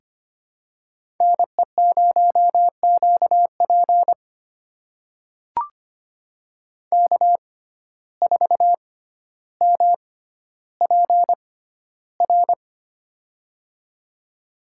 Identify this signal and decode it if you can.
Morse code, heard as a keyed tone, about 25 wpm, spaced as for 13 wpm, 700 Hz. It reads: NE0QP K4MPR
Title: USA Call Signs - 5 Characters - 6x Character Spacing - 25wpm